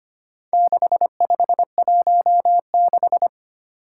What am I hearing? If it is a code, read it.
Morse code, heard as a keyed tone, about 25 wpm, 700 Hz. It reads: 6516
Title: Four Challenging Letter-Number Combinations 25wpm